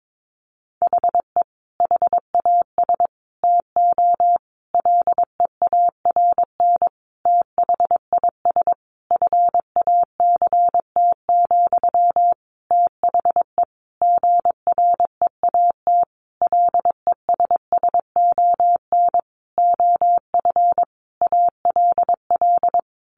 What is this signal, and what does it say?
Morse code, heard as a keyed tone, about 22 wpm, 700 Hz. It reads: HE HAS TO LEARN THIS FACT, THE GREAT LESSON OF ALL
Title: Sentences from Top 1200 Words 22wpm